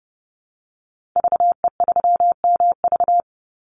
Morse code, heard as a keyed tone, about 30 wpm, 700 Hz. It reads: VE3MV
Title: Call Signs Repeated 3x - 30wpm